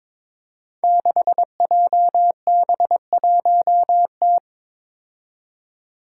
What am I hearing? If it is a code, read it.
Morse code, heard as a keyed tone, about 22 wpm, 700 Hz. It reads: 6JB1T
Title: Five Challenging Letter-Number Combinations 22wpm